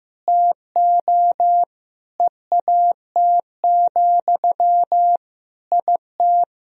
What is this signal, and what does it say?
Morse code, heard as a keyed tone, about 15 wpm, 700 Hz. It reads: TO EAT, IT
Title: Sentences from Top 1000 Words - Review Part 1 - 15wpm